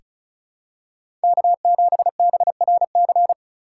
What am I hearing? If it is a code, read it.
Morse code, heard as a keyed tone, about 34 wpm, 700 Hz. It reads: K7BRC
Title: USA Call Signs - 5 Characters - 4x Character Spacing - 35wpm